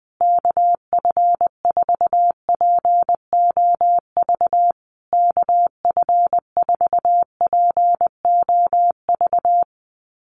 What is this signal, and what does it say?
Morse code, heard as a keyed tone, about 20 wpm, 700 Hz. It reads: KF4POV KF4POV
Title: USA Call Signs Repeated 3x 20wpm